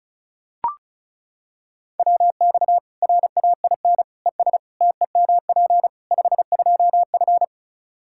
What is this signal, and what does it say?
Morse code, heard as a keyed tone, about 36 wpm, 700 Hz. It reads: WX RAIN ES TEMP 52F
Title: Practice QSO - Exchange 2 - Info - 35wpm